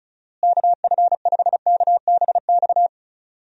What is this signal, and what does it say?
Morse code, heard as a keyed tone, about 36 wpm, 700 Hz. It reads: KF5KBX
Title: USA Call Signs - 6 Characters - Repeated 2x - 35wpm